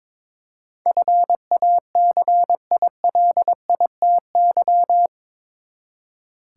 Morse code, heard as a keyed tone, about 22 wpm, 700 Hz. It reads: FACILITY
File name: practice / 22wpm / Eight Letter Words - 22wpm